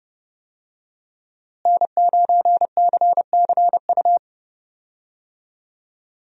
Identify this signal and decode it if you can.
Morse code, heard as a keyed tone, about 30 wpm, 700 Hz. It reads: N9CCU
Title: USA Call Signs - 5 Characters - Repeated 3x - 30wpm